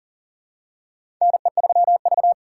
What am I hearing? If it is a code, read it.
Morse code, heard as a keyed tone, about 37 wpm, 700 Hz. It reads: NE3V
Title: USA Call Signs - 4 Characters - 3x Character Spacing - 40wpm